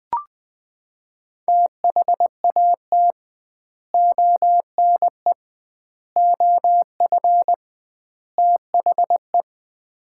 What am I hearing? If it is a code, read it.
Morse code, heard as a keyed tone, about 20 wpm, 700 Hz. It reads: THAT ONE OF THE
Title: 2x Word-Spacing - Sets of 4 Words - 20wpm